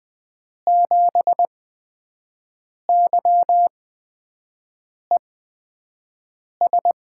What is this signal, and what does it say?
Morse code, heard as a keyed tone, about 20 wpm, spaced as for 5 wpm, 700 Hz. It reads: 7YES
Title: USA Call Signs - 6 Characters - 8x Character Spacing - 20wpm